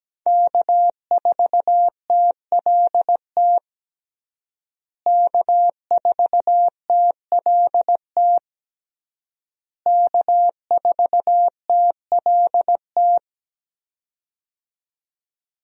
Morse code, heard as a keyed tone, about 17 wpm, 700 Hz. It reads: K4TLT K4TLT K4TLT
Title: USA Call Signs - 5 Characters - Repeated 3x - 17wpm